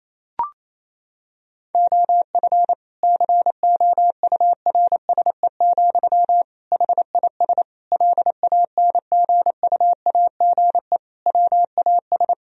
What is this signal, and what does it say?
Morse code, heard as a keyed tone, about 27 wpm, 700 Hz. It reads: OF COURSE, HIS LANGUAGE WAS
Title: Sentences from Top 1000 Words - Review Part 3 - 28wpm